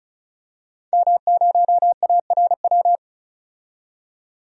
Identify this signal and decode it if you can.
Morse code, heard as a keyed tone, about 36 wpm, 700 Hz. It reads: M0ARW
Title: DXCC-100 Call Signs - 5 Characters - 35wpm